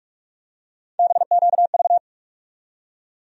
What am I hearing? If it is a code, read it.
Morse code, heard as a keyed tone, about 42 wpm, 700 Hz. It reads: BQV
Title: USA Call Sign Postfixes - 3 Letters 45wpm